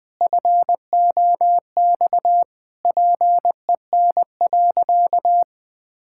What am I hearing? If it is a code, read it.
Morse code, heard as a keyed tone, about 20 wpm, 700 Hz. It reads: FOX PEN.
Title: Pangram Challenge v2 - 20wpm